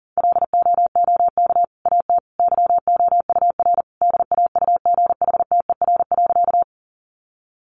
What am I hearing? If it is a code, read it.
Morse code, heard as a keyed tone, about 40 wpm, 700 Hz. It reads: LOOK AT YOUR DAUGHTER.
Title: Sentences from Top 1500 Words 40wpm